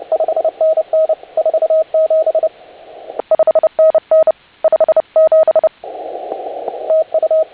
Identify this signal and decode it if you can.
Morse code, heard as a keyed tone, about 29 wpm, 630 Hz. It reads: 5NN 47 5NN 57 TU